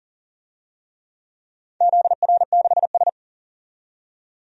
Morse code, heard as a keyed tone, about 40 wpm, 700 Hz. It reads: ZR6S
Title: DXCC-100 Call Signs - 4 Characters - 40wpm